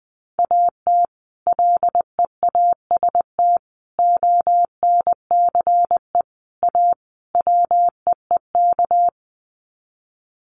Morse code, heard as a keyed tone, about 20 wpm, 700 Hz. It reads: AT LEAST ONCE A WEEK
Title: Sets of 5 Words Encore2 20wpm